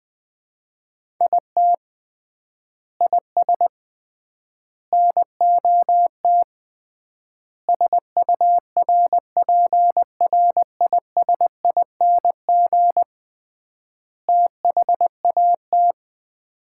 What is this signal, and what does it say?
Morse code, heard as a keyed tone, about 20 wpm, 700 Hz. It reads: IT IS NOT SURPRISING THAT
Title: 3x Word-Spacing - Sets of 5 Words - 20wpm